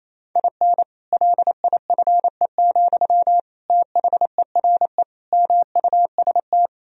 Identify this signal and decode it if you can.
Morse code, heard as a keyed tone, about 28 wpm, 700 Hz. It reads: IN LIFE, THERE MUST
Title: Sentences from Top 1000 Words - Review Part 4 - 28wpm